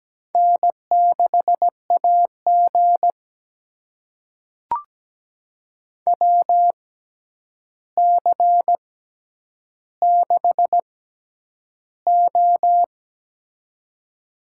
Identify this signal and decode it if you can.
Morse code, heard as a keyed tone, about 17 wpm, spaced as for 9 wpm, 700 Hz. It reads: N6AG WC6O
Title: USA Call Signs - 4 Characters - 6x Character Spacing - 17wpm